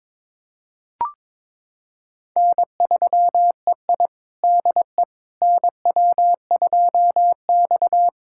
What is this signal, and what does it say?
Morse code, heard as a keyed tone, about 22 wpm, 700 Hz. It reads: N3EI DE NW2X
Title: USA Callsign DE Callsign - 4 Characters - 22wpm